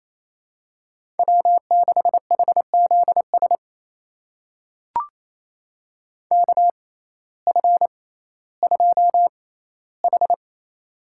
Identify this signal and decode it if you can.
Morse code, heard as a keyed tone, about 28 wpm, spaced as for 15 wpm, 700 Hz. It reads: W6HZS KF2H
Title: USA Call Signs - 6x Character Spacing - 28wpm